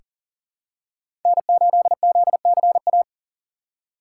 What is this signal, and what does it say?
Morse code, heard as a keyed tone, about 40 wpm, 700 Hz. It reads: N8ZCA
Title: USA Call Signs - 5 Characters - 2x Character Spacing - 40wpm